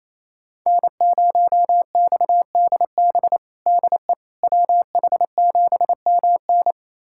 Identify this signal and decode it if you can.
Morse code, heard as a keyed tone, about 27 wpm, 700 Hz. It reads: N0XDB DE WH7MN
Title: USA Callsign DE Callsign - 5 Characters - 28wpm